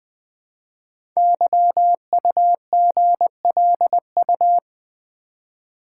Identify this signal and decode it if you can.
Morse code, heard as a keyed tone, about 20 wpm, 700 Hz. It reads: YUGLU
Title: Code Groups - 5 Letter:Numbers - 20wpm